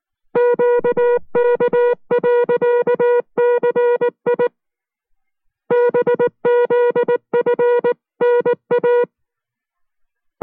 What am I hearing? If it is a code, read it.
Morse code, heard as a keyed tone, about 19 wpm, 480 Hz. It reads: QK.CI BZFNA